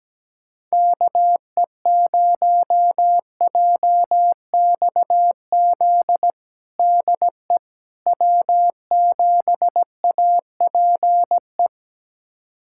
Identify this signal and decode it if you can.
Morse code, heard as a keyed tone, about 17 wpm, 700 Hz. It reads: KE0JXZ DE W7APE
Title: USA Callsign DE Callsign 17wpm